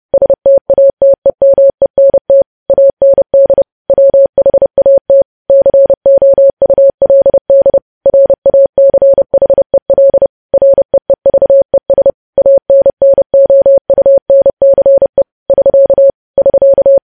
Morse code, heard as a keyed tone, about 30 wpm, 550 Hz. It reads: STATEMENT AND WHAT COULD RACHEL REEVES ANNOUNCE <SK> <SK>
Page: news headlines in morse code at 30 wpm